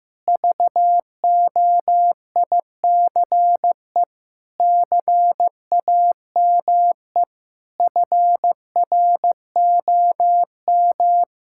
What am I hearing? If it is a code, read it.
Morse code, heard as a keyed tone, about 15 wpm, 700 Hz. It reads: VOICE CAME FROM
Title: Sentences from Top 1000 Words - Review Part 6 - 15wpm